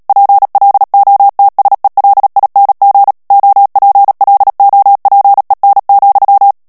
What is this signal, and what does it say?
Morse code, heard as a keyed tone, about 37 wpm, 800 Hz. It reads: PLOTSELING OPLOPEN,
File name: morse mp3 / HOLLANDAIS texte clair 2 vitesse 35 mots minute